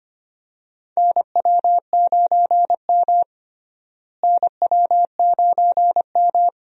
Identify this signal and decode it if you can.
Morse code, heard as a keyed tone, about 25 wpm, 700 Hz. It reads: NW9M NW9M
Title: USA Call Signs - 4 Characters - Repeated 3x - 25wpm